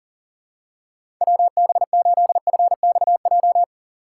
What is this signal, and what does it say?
Morse code, heard as a keyed tone, about 37 wpm, 700 Hz. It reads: WB8FXJ
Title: K1USN SST QSO Practice - 40wpm